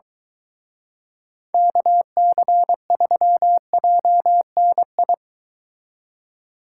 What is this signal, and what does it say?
Morse code, heard as a keyed tone, about 23 wpm, 700 Hz. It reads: KC3JNI